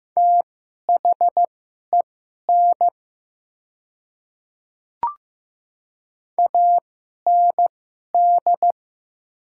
Morse code, heard as a keyed tone, about 15 wpm, spaced as for 11 wpm, 700 Hz. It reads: THEN AND
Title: 2x Character Spacing - Top 100 Words - 15wpm